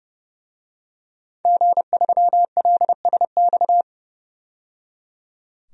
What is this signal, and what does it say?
Morse code, heard as a keyed tone, about 30 wpm, 700 Hz. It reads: G3LSX
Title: DXCC-100 Call Signs - 5 Characters - 30wpm